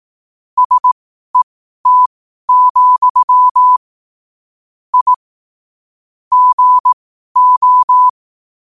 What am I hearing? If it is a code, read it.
Morse code, heard as a keyed tone, about 18 wpm, spaced as for 12 wpm, 1000 Hz. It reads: SET, I GO
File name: qotdfiles / 12wpm